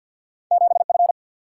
Morse code, heard as a keyed tone, about 50 wpm, 700 Hz. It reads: 7 F
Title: Single-Letter Single-Number 50wpm